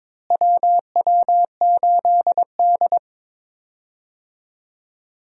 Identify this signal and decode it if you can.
Morse code, heard as a keyed tone, about 22 wpm, 700 Hz. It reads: WW8D
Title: USA Call Signs - 4 Characters - Repeated 2x - 22wpm